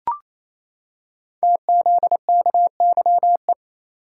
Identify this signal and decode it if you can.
Morse code, heard as a keyed tone, about 29 wpm, 700 Hz. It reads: TZKYE
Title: Code Groups - 5 Letters - 28wpm